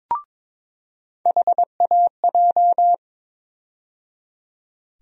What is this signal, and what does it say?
Morse code, heard as a keyed tone, about 22 wpm, 700 Hz. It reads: HAJ
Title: USA Call Sign Postfixes - 3 Letters 22wpm